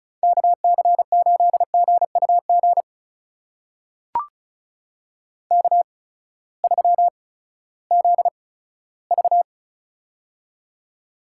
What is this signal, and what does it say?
Morse code, heard as a keyed tone, about 36 wpm, spaced as for 18 wpm, 700 Hz. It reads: KC8GUG K3ZV
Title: USA Call Signs - 8x Character Spacing - 35wpm